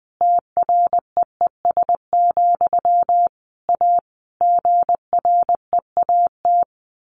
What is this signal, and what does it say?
Morse code, heard as a keyed tone, about 20 wpm, 700 Hz. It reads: TREES, A GREAT
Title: A Sound Of Thunder 20wpm